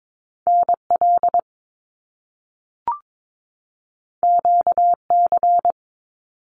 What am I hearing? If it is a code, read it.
Morse code, heard as a keyed tone, about 22 wpm, 700 Hz. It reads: NL QC
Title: Canadian Provinces and Territories 22wpm